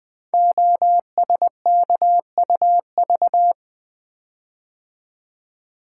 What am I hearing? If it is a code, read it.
Morse code, heard as a keyed tone, about 20 wpm, 700 Hz. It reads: OSKUV
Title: Code Groups - 5 Letters - 20wpm